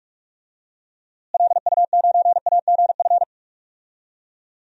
Morse code, heard as a keyed tone, about 42 wpm, 700 Hz. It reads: LU9AGF
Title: DXCC-100 Call Signs - 6 Characters - Repeated 2x - 45wpm